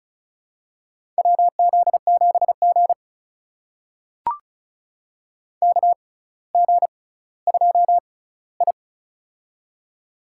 Morse code, heard as a keyed tone, about 36 wpm, spaced as for 18 wpm, 700 Hz. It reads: WZ7G KG2I
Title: USA Call Signs - 4 Characters - 6x Character Spacing - 35wpm